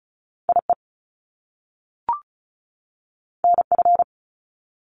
Morse code, heard as a keyed tone, about 36 wpm, 700 Hz. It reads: IE NF
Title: Top Two Letter Combinations in Words 35wpm